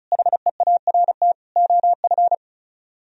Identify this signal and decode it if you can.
Morse code, heard as a keyed tone, about 42 wpm, 700 Hz. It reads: HEART OF
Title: Speed-Racing - Sets of 4 Words 35wpm